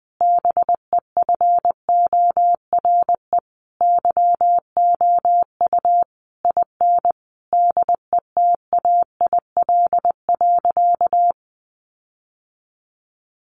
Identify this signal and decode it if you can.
Morse code, heard as a keyed tone, about 20 wpm, 700 Hz. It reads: BEFORE YOU IN DETAIL.
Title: Sentences from Top 1500 Words 20wpm